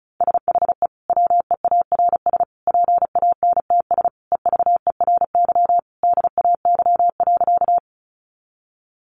Morse code, heard as a keyed tone, about 34 wpm, 700 Hz. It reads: SHE WEARS PANTS EVERY DAY.
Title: Additional Easy Phrases 35wpm